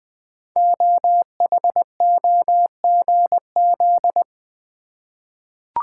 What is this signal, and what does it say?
Morse code, heard as a keyed tone, about 20 wpm, 700 Hz. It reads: OHOGZ
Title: Code Groups - 5 Letters - 20wpm